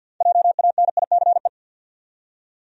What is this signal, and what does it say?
Morse code, heard as a keyed tone, about 49 wpm, 700 Hz. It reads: JANICE
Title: USA Common Names - 50wpm